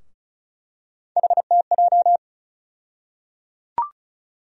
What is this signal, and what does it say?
Morse code, heard as a keyed tone, about 34 wpm, 700 Hz. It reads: HTJ
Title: USA Call Sign Postfixes - 3 Letters 35wpm